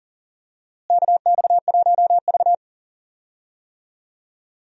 Morse code, heard as a keyed tone, about 40 wpm, 700 Hz. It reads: KX1V
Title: USA Call Signs - 4 Characters - Repeated 3x - 40wpm